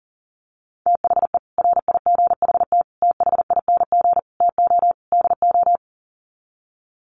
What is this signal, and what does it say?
Morse code, heard as a keyed tone, about 40 wpm, 700 Hz. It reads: THE RIGHT THING TO DO
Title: Sets of 5 Words 40wpm